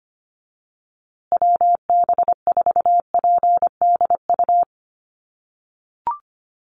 Morse code, heard as a keyed tone, about 25 wpm, 700 Hz. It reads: WB4PDU
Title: USA Call Signs Repeated 2x 25wpm